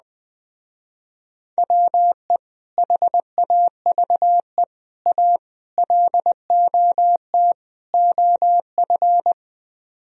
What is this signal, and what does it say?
Morse code, heard as a keyed tone, about 20 wpm, 700 Hz. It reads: WE HAVE A LOT OF